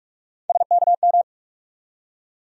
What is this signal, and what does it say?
Morse code, heard as a keyed tone, about 46 wpm, 700 Hz. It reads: SKM
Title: USA Call Sign Postfixes - 3 Letters 45wpm